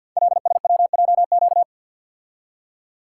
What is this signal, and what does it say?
Morse code, heard as a keyed tone, about 52 wpm, 700 Hz. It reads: LSWJQ
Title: Code Groups - 5 Letter:Numbers - 50wpm